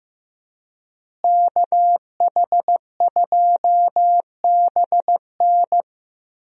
Morse code, heard as a keyed tone, about 15 wpm, 700 Hz. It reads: KH2BN
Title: USA Call Signs - 5 Characters - Repeated 2x - 15wpm